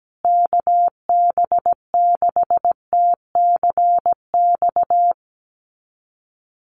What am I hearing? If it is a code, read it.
Morse code, heard as a keyed tone, about 17 wpm, 700 Hz. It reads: KB6TCX
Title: USA Call Signs Repeated 2x 17wpm